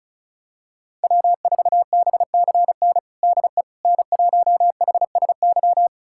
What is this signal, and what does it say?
Morse code, heard as a keyed tone, about 36 wpm, 700 Hz. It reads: W4BCN DE N1HSY
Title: USA Callsign DE Callsign - 5 Characters - 35wpm